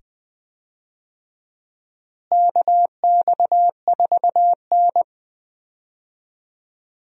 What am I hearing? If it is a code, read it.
Morse code, heard as a keyed tone, about 20 wpm, 700 Hz. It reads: KX4N
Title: USA Call Signs - 4 Characters - Repeated 2x - 20wpm